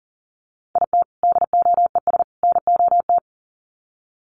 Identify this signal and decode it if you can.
Morse code, heard as a keyed tone, about 40 wpm, 700 Hz. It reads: IT DOES NOT
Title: Sets of 3 Words 40wpm